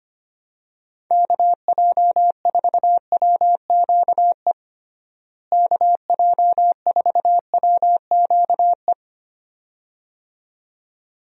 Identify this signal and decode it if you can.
Morse code, heard as a keyed tone, about 25 wpm, 700 Hz. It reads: KJ4WQE KJ4WQE
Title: USA Call Signs - 6 Characters - Repeated 2x - 25wpm